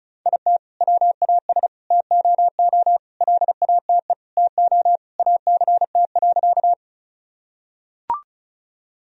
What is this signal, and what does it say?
Morse code, heard as a keyed tone, about 36 wpm, 700 Hz. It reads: IT WAS TOO LATE TO ACT.
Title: Sentences from Top 1000 Words - Review Part 1 - 35wpm